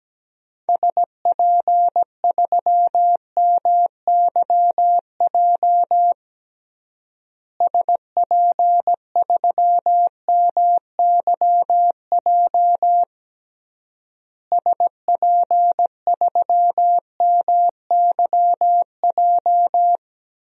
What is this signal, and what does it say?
Morse code, heard as a keyed tone, about 17 wpm, 700 Hz. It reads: SP3MYJ SP3MYJ SP3MYJ
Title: DXCC-100 Call Signs - 6 Characters - Repeated 3x - 17wpm